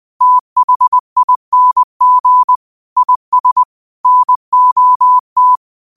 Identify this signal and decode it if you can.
Morse code, heard as a keyed tone, about 20 wpm, 1000 Hz. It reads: THING IS NOT
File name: qotdfiles / 20wpm